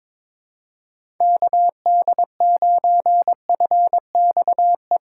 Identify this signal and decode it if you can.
Morse code, heard as a keyed tone, about 22 wpm, 700 Hz. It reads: KD9FXE
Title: USA Call Signs - 3x Character Spacing - 22wpm